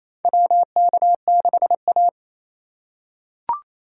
Morse code, heard as a keyed tone, about 28 wpm, 700 Hz. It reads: WK6A
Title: USA Call Signs - 4 Characters - 3x Character Spacing - 28wpm